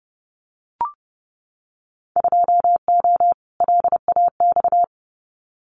Morse code, heard as a keyed tone, about 30 wpm, 700 Hz. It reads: 2O LAX
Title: Winter Field Day Exchanges 30wpm